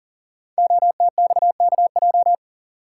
Morse code, heard as a keyed tone, about 38 wpm, 700 Hz. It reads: OTXKJ
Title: Code Groups - 5 Letter:Numbers - 40wpm